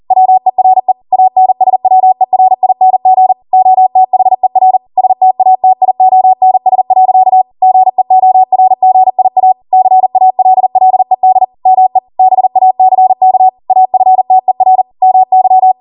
Morse code, heard as a keyed tone, about 40 wpm, 750 Hz. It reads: WERE ANSWERING OTHER STATIONS. GEORGIA CALLED ME BACK AFTER MY